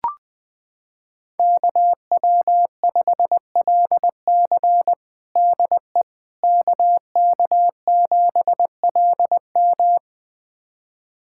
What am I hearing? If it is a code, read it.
Morse code, heard as a keyed tone, about 20 wpm, 700 Hz. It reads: KW5LC DE KK7LM
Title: USA Callsign DE Callsign - 5 Characters - 20wpm